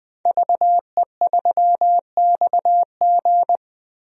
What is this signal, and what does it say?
Morse code, heard as a keyed tone, about 20 wpm, 700 Hz. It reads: VE3XG
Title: DXCC-100 Call Signs - 5 Characters - 20wpm